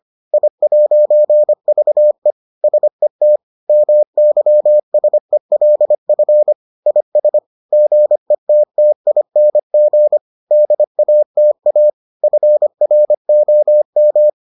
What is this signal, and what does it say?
Morse code, heard as a keyed tone, about 25 wpm, 600 Hz. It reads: I'VE SET MYSELF IS GETTING DATA FROM